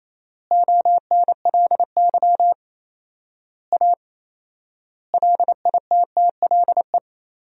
Morse code, heard as a keyed tone, about 27 wpm, 700 Hz. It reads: ONLY A LITTLE
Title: 4x Word-Spacing - Sentences from Top 400 Words - 28wpm